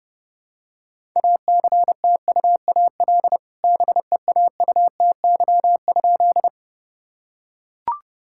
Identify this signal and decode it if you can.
Morse code, heard as a keyed tone, about 30 wpm, 700 Hz. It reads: ACTUAL BEAUTY?
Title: Sentences from Top 1000 Words - Review Part 5 - 30wpm